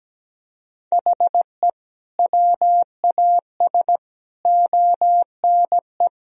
Morse code, heard as a keyed tone, about 17 wpm, 700 Hz. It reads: HE WAS ONE
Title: Sentences from Top 1000 Words - Review Part 1 - 17wpm